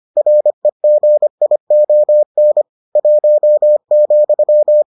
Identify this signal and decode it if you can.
Morse code, heard as a keyed tone, about 25 wpm, 600 Hz. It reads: REGION 1,